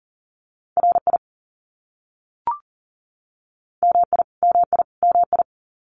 Morse code, heard as a keyed tone, about 40 wpm, 700 Hz. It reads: RI MI MI MI
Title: US States Repeated 3x 40wpm